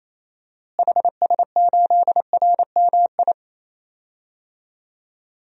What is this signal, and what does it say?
Morse code, heard as a keyed tone, about 28 wpm, 700 Hz. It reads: HS8RMI
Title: DXCC-100 Call Signs - 6 Characters - Repeated 3x - 28wpm